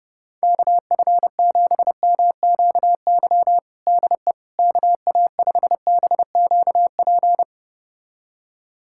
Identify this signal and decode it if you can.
Morse code, heard as a keyed tone, about 32 wpm, 700 Hz. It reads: KF7MQY DE KA5BQP
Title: USA Callsign DE Callsign - 6 Characters - 30wpm